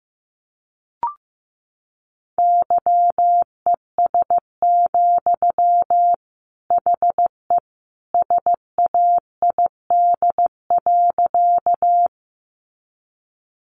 Morse code, heard as a keyed tone, about 15 wpm, 700 Hz. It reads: YES, HE SAID.
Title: Sentences from Top 500 Words 15wpm